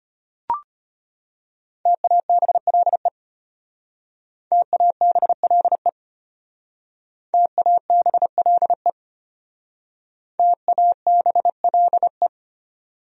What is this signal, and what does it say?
Morse code, heard as a keyed tone, about 31 wpm, 700 Hz. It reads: TABLE TABLE TABLE TABLE
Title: Speed-Racing Top 500 Words 25wpm